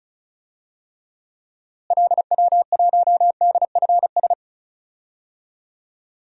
Morse code, heard as a keyed tone, about 34 wpm, 700 Hz. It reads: LW1DFS
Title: DXCC-100 Call Signs - 6 Characters - Repeated 2x - 35wpm